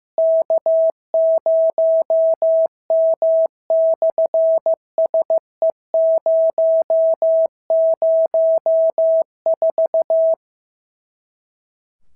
Morse code, heard as a keyed tone, about 16 wpm, 650 Hz. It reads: K0M/SE004
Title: SOTA QSO Practice 15wpm